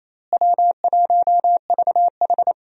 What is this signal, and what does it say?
Morse code, heard as a keyed tone, about 27 wpm, 700 Hz. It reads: W1VH
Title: USA Call Signs - 4 Characters - Repeated 3x - 28wpm